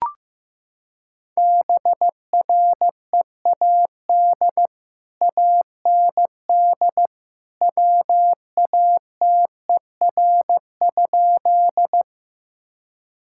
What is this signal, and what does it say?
Morse code, heard as a keyed tone, about 14 wpm, 700 Hz. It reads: BREAD AND WATER?
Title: Sentences from Top 1600 Words 15wpm